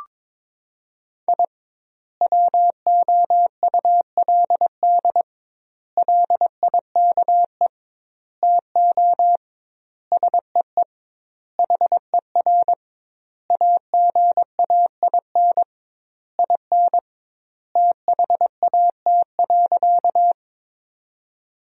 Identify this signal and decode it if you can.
Morse code, heard as a keyed tone, about 21 wpm, 700 Hz. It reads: I WOULD LIKE TO SEE HER AGAIN IN THAT.
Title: 2x Word-Spacing - Sentences from Top 200 Words - 22wpm